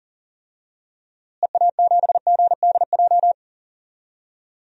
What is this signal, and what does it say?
Morse code, heard as a keyed tone, about 37 wpm, 700 Hz. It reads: EA7GDJ
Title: DXCC-100 Call Signs - 6 Characters - Repeated 3x - 40wpm